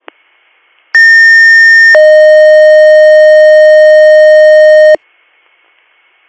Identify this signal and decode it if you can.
Two-tone page, heard as a one-second tone, then a three-second tone, about 1800 then 620 Hz. It is heard from 0.9 to 5.0 s.